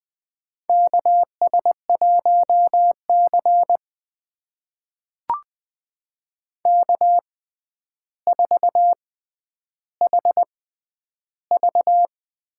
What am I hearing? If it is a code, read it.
Morse code, heard as a keyed tone, about 20 wpm, spaced as for 10 wpm, 700 Hz. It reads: KS1C K4HV
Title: USA Call Signs - 6x Character Spacing - 20wpm